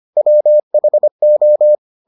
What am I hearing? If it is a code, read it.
Morse code, heard as a keyed tone, about 25 wpm, 600 Hz. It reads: WHO